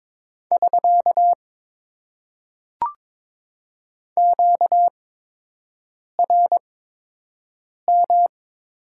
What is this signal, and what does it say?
Morse code, heard as a keyed tone, about 22 wpm, spaced as for 6 wpm, 700 Hz. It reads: <SK> QRM
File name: Top 90 QSO Elements - 8x Character Spacing - 22wpm